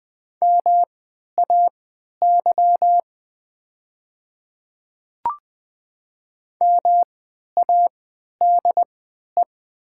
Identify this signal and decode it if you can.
Morse code, heard as a keyed tone, about 20 wpm, spaced as for 11 wpm, 700 Hz. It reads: MAY MADE
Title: 3x Character Spacing - Top 300 Words - 20wpm